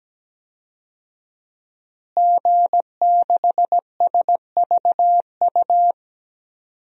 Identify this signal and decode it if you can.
Morse code, heard as a keyed tone, about 17 wpm, 700 Hz. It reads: G6SVU
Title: DXCC-100 Call Signs - 5 Characters - Repeated 3x - 17wpm